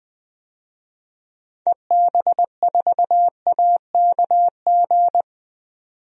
Morse code, heard as a keyed tone, about 20 wpm, 700 Hz. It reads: EB4AKG
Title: DXCC-100 Call Signs - 6 Characters - 20wpm